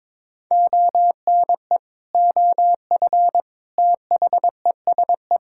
Morse code, heard as a keyed tone, about 22 wpm, 700 Hz. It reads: ONE OF THESE
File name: practice / 22wpm / Sets of 8 Words - 22wpm